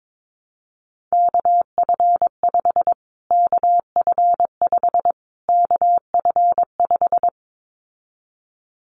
Morse code, heard as a keyed tone, about 22 wpm, 700 Hz. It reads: KF5 KF5 KF5
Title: Callsign Prefixes for Top 25 DXCC - Repeated 3x - 22wpm